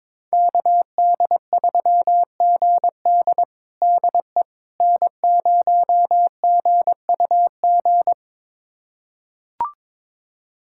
Morse code, heard as a keyed tone, about 22 wpm, 700 Hz. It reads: KD3GD DE N0GUG
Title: USA Callsign DE Callsign - 5 Characters - 22wpm